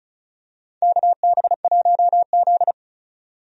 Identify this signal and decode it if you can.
Morse code, heard as a keyed tone, about 36 wpm, 700 Hz. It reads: KB1Z
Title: USA Call Signs - 4 Characters - 35wpm